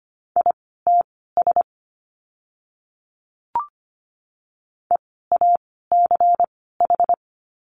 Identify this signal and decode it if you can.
Morse code, heard as a keyed tone, about 25 wpm, spaced as for 16 wpm, 700 Hz. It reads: ITS EACH
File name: course / 25:10wpm / 094 Lesson 16f - Letter c - Words 25wpm